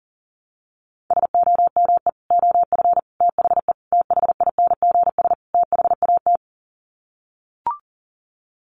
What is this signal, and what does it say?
Morse code, heard as a keyed tone, about 37 wpm, 700 Hz. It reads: SOME OF THE THINGS THAT
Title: Sets of 5 Words 40wpm